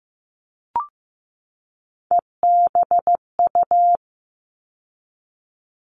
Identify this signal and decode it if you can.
Morse code, heard as a keyed tone, about 15 wpm, 700 Hz. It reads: EBU
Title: All Cut Numbers - 3-Digit 15wpm